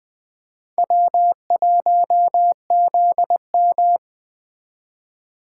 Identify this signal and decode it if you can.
Morse code, heard as a keyed tone, about 20 wpm, 700 Hz. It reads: W1ZM
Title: Speed Racing - USA 1x2 Call Signs 17wpm